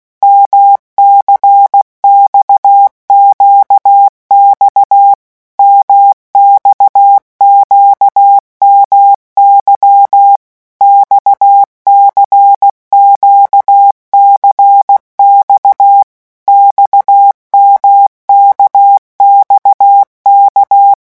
Morse code, heard as a keyed tone, about 16 wpm, 800 Hz. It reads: MCXQX MXQMY XCQCX XMKXK
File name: morse mp3 / revision1-vitesse 15 mots